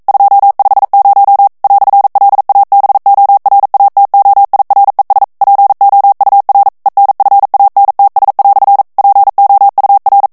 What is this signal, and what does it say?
Morse code, heard as a keyed tone, about 42 wpm, 800 Hz. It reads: 250 "LABORATOIRES POUR ENFANTS" POUR
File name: FRANCAIS TEXTE CLAIR 1 VITESSE 40 MOTS MINUTES